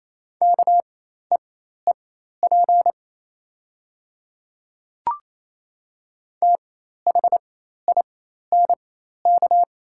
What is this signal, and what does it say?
Morse code, heard as a keyed tone, about 28 wpm, spaced as for 13 wpm, 690 Hz. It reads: KEEP THINK
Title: 4x Character Spacing - Top 300 Words - 28wpm